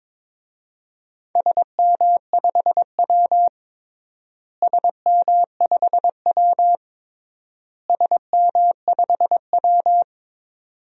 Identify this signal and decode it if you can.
Morse code, heard as a keyed tone, about 21 wpm, 700 Hz. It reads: SM5W SM5W SM5W
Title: DXCC-100 Call Signs - 4 Characters - Repeated 3x - 22wpm